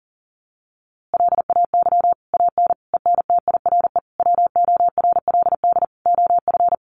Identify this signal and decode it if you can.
Morse code, heard as a keyed tone, about 40 wpm, 700 Hz. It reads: LAY AN ENTIRE WORLD OF
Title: A Sound Of Thunder 40wpm